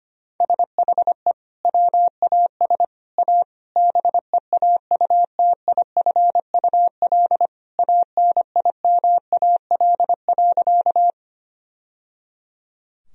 Morse code, heard as a keyed tone, about 25 wpm, 700 Hz. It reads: SHE WAS A BEAUTIFUL ANIMAL.
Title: Sentences from Top 1000 Words - Review Part 1 - 25wpm